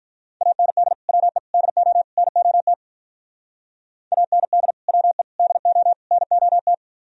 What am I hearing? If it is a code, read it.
Morse code, heard as a keyed tone, about 45 wpm, 700 Hz. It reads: AND WE DO NOT AND WE DO NOT